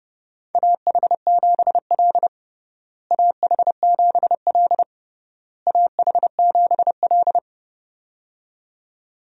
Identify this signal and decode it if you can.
Morse code, heard as a keyed tone, about 30 wpm, 700 Hz. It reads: AH7L AH7L AH7L
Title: USA Call Signs - 4 Characters - Repeated 3x - 30wpm